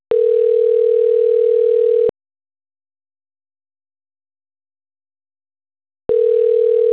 Ringback tone (North America), 440 + 480 Hz.
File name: autotestagent / 11-16-22.136..callcentertransfer. .d133dfe6-69a1-4e8e-b24e